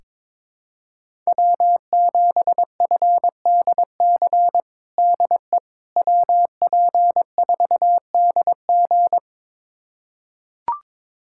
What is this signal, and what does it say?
Morse code, heard as a keyed tone, about 22 wpm, 700 Hz. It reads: W7FDC DE WP4DG
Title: USA Callsign DE Callsign - 5 Characters - 22wpm